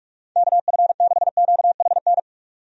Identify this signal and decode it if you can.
Morse code, heard as a keyed tone, about 44 wpm, 700 Hz. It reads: KF6QHN
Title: USA Call Signs - 6 Characters - 45wpm